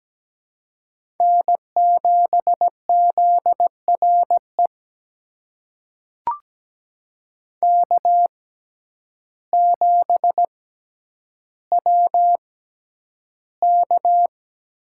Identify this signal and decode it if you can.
Morse code, heard as a keyed tone, about 17 wpm, spaced as for 9 wpm, 700 Hz. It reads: N7ZRE K7WK
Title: USA Call Signs - 5 Characters - 6x Character Spacing - 17wpm